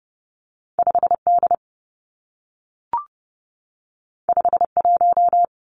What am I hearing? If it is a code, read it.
Morse code, heard as a keyed tone, about 30 wpm, 700 Hz. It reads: 5 D 5 1
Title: Two Challenging Letter-Number Combinations 30wpm